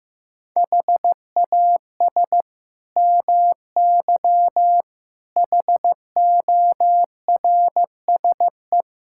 Morse code, heard as a keyed tone, about 15 wpm, 700 Hz. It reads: HAS MY HORSE
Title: Sentences from Top 1600 Words 15wpm